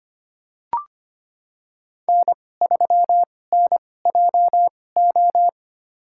Morse code, heard as a keyed tone, about 25 wpm, spaced as for 18 wpm, 700 Hz. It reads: N3NJO
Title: USA Call Signs - 5 Characters - 2x Character Spacing - 25wpm